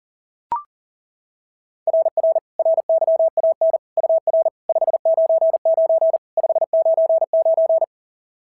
Morse code, heard as a keyed tone, about 40 wpm, 690 Hz. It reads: RR RYAN UR 599 599
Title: SOTA QSO Practice 40wpm